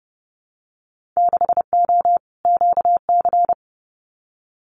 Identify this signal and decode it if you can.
Morse code, heard as a keyed tone, about 32 wpm, 700 Hz. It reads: 6O QC
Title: Winter Field Day Exchanges Encore 30wpm